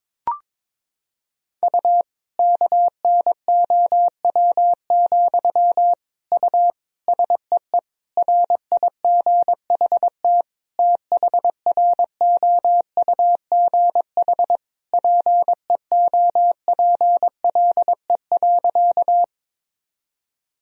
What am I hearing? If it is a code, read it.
Morse code, heard as a keyed tone, about 22 wpm, 700 Hz. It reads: U KNOW, U SEE RIGHT THROUGH PEOPLE.
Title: Sentences from Top 200 Words and Abbreviations 22wpm